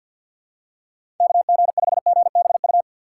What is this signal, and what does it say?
Morse code, heard as a keyed tone, about 53 wpm, 700 Hz. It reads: KG5GBU
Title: USA Call Signs - 3x Character Spacing - 50wpm